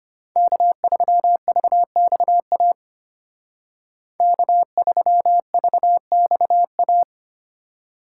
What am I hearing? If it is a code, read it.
Morse code, heard as a keyed tone, about 30 wpm, 700 Hz. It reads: K3VXA K3VXA
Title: Speed Racing - USA 1x3 Call Signs 25wpm